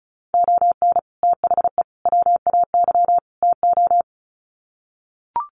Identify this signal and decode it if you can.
Morse code, heard as a keyed tone, about 34 wpm, 700 Hz. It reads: ON THE WAY TO
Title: Sets of 4 Words 35wpm